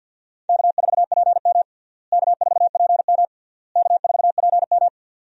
Morse code, heard as a keyed tone, about 50 wpm, 700 Hz. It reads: K4PM K4PM K4PM
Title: USA Call Signs - 4 Characters - Repeated 3x - 50wpm